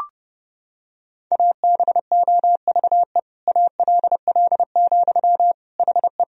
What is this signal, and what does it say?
Morse code, heard as a keyed tone, about 32 wpm, 700 Hz. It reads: ABOVE ALL, HE
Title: Sentences from Top 1000 Words - Review Part 2 - 30wpm